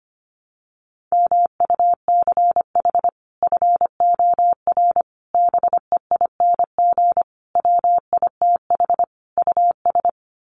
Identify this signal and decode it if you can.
Morse code, heard as a keyed tone, about 25 wpm, 700 Hz. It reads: MUCH FOR BEING WITH US
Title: Sets of 5 Words 25wpm